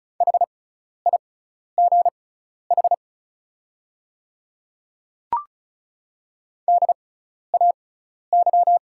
Morse code, heard as a keyed tone, about 36 wpm, spaced as for 12 wpm, 700 Hz. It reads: HIGH DAY
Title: 6x Character Spacing - Top 200 Words - 35wpm